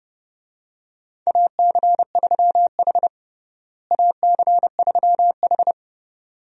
Morse code, heard as a keyed tone, about 30 wpm, 700 Hz. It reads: AC3H AC3H
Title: USA Call Signs - 4 Characters - Repeated 2x - 30wpm